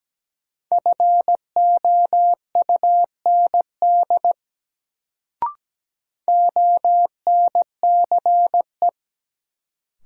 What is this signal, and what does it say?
Morse code, heard as a keyed tone, about 17 wpm, 700 Hz. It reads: FOUND ONCE
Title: Sentences from Top 300 Words - New Word Drill - 17wpm